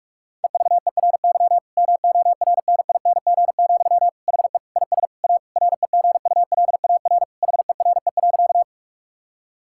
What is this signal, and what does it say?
Morse code, heard as a keyed tone, about 46 wpm, 700 Hz. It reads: EVERY MORNING, HE IS A REGULAR HERE.
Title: Sentences from Top 1000 Words - Review Part 6 - 45wpm